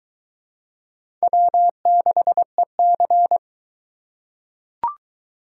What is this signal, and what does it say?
Morse code, heard as a keyed tone, about 22 wpm, 700 Hz. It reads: W6EC